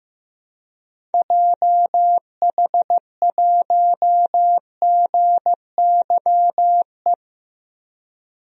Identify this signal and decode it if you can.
Morse code, heard as a keyed tone, about 15 wpm, 700 Hz. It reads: JH1GYE
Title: DXCC-100 Call Signs - 6 Characters - 15wpm